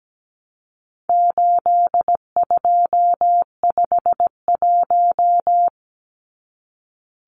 Morse code, heard as a keyed tone, about 17 wpm, 700 Hz. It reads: 8251
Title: Numbers - 4-Digit 17wpm